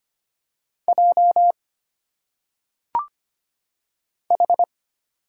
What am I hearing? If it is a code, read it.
Morse code, heard as a keyed tone, about 25 wpm, 700 Hz. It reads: J H
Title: Single Letters - Phonetic Pronunciation 25wpm